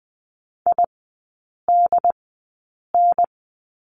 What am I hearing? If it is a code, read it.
Morse code, heard as a keyed tone, about 20 wpm, spaced as for 8 wpm, 700 Hz. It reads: IDN
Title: 067 Lesson 12e - Letter d - Three Characters 20wpm